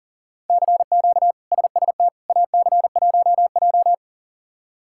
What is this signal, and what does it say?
Morse code, heard as a keyed tone, about 37 wpm, 700 Hz. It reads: CQ SST AC1J
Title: K1USN SST QSO Practice - 40wpm